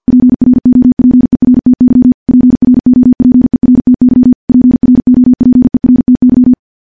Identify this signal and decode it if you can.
Morse code, heard as a keyed tone, about 50 wpm, 250 Hz. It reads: PROPERTY PROPERTY PROPERTY
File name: morse / morse-training-50WPM